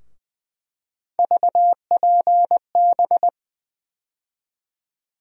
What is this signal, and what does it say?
Morse code, heard as a keyed tone, about 20 wpm, 700 Hz. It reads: VPB
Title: USA Call Sign Postfixes - 3 Letters 20wpm